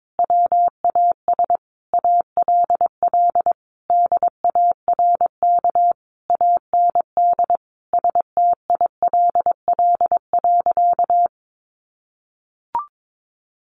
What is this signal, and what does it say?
Morse code, heard as a keyed tone, about 22 wpm, 700 Hz. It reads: WAS ALL DARK AND STILL.
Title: Sentences from Top 500 Words 22wpm